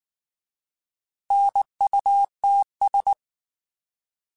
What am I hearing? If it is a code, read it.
Morse code, heard as a keyed tone, about 19 wpm, 800 Hz. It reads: NUTS
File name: Alphabetical / Alphabetical mp3s / Ns @18